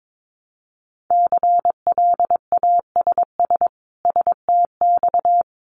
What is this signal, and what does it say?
Morse code, heard as a keyed tone, about 22 wpm, 700 Hz. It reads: CLASS STX
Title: Winter Field Day Exchanges 22wpm